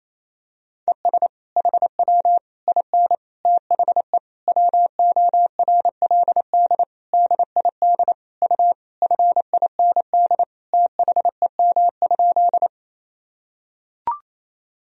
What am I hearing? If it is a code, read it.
Morse code, heard as a keyed tone, about 28 wpm, 700 Hz. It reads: ES HW IN THE WORLD DID U FIND THEM?
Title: Sentences from Top 200 Words and Abbreviations 28wpm